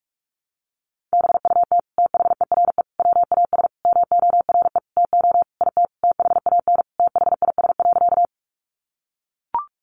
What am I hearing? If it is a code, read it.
Morse code, heard as a keyed tone, about 44 wpm, 700 Hz. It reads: BUT THERE WAS MORE TO IT THAN THIS.
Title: Sentences from Top 100 Words 45wpm